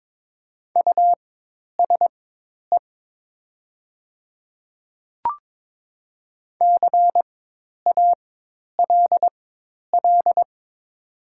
Morse code, heard as a keyed tone, about 22 wpm, spaced as for 10 wpm, 700 Hz. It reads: USE CALL
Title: 4x Character Spacing - Top 100 Words - 22wpm